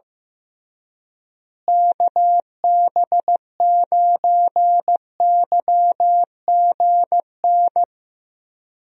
Morse code, heard as a keyed tone, about 16 wpm, 700 Hz. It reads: KB9YGN
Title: USA Call Signs - 6 Characters - Repeated 3x - 15wpm